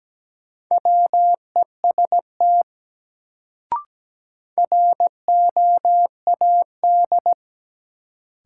Morse code, heard as a keyed tone, about 17 wpm, 700 Hz. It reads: WEST ROAD
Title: Sets of 5 Words - New Word Drill - 17wpm